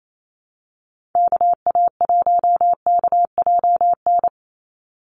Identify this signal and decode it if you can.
Morse code, heard as a keyed tone, about 28 wpm, 700 Hz. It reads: KA1KJN
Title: USA Call Signs 28wpm